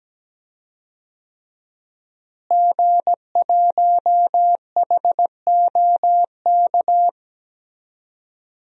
Morse code, heard as a keyed tone, about 17 wpm, 700 Hz. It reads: G1HOK
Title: DXCC-100 Call Signs - 5 Characters - Repeated 3x - 17wpm